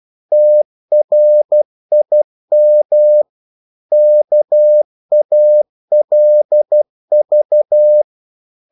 Morse code, heard as a keyed tone, about 12 wpm, 600 Hz. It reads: TRIM KALV
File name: Korte ord DK 12wpm